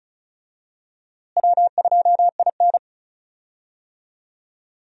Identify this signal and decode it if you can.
Morse code, heard as a keyed tone, about 34 wpm, 700 Hz. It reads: W2IN
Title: USA Call Signs - 4 Characters - Repeated 2x - 35wpm